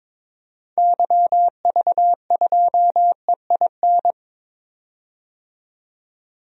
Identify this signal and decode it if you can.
Morse code, heard as a keyed tone, about 22 wpm, 700 Hz. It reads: YV2EIN
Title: DXCC-100 Call Signs - 6 Characters - 22wpm